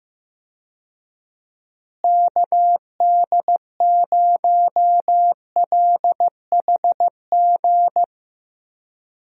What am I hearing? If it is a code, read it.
Morse code, heard as a keyed tone, about 15 wpm, 700 Hz. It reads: KD0LHG
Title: USA Call Signs - 6 Characters - Repeated 2x - 15wpm